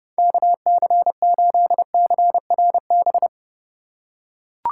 Morse code, heard as a keyed tone, about 32 wpm, 700 Hz. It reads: KC8CRB
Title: USA Call Signs - 6 Characters - 3x Character Spacing - 30wpm